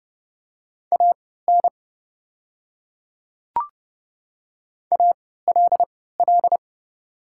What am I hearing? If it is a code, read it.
Morse code, heard as a keyed tone, about 32 wpm, spaced as for 17 wpm, 700 Hz. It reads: AN ALL
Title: 3x Character Spacing - Top 100 Words - 30wpm